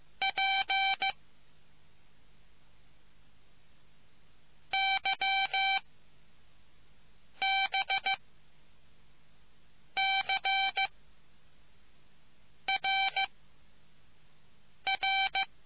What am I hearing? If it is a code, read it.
Morse code, heard as a keyed tone, about 15 wpm, 780 Hz. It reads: P Y B C R R